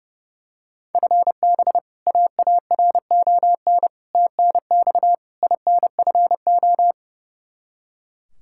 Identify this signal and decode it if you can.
Morse code, heard as a keyed tone, about 32 wpm, 700 Hz. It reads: FB AARON TNX INFO
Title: Practice QSO - Exchange 3 - More Info - 30wpm